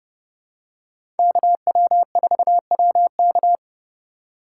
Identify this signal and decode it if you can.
Morse code, heard as a keyed tone, about 30 wpm, 700 Hz. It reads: KW4WK
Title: USA Call Signs - 5 Characters - 3x Character Spacing - 30wpm